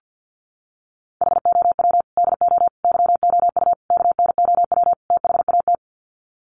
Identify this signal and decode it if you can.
Morse code, heard as a keyed tone, about 50 wpm, 700 Hz. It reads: HOW DO YOU KNOW THAT
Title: Sets of 5 Words Encore 50wpm